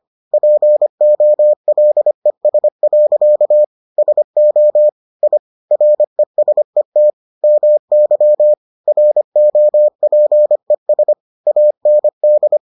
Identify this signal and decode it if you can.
Morse code, heard as a keyed tone, about 25 wpm, 600 Hz. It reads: POLES. SO I RESET MY ROPES AND